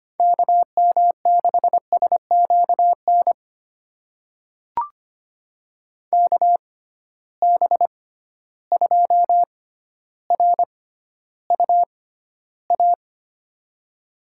Morse code, heard as a keyed tone, about 25 wpm, spaced as for 13 wpm, 700 Hz. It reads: KM6SQN KB2RUA
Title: USA Call Signs - 6 Characters - 6x Character Spacing - 25wpm